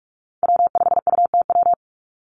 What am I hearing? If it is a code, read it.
Morse code, heard as a keyed tone, about 44 wpm, 700 Hz. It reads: W5UTW
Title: Call Signs Repeated 3x - 45wpm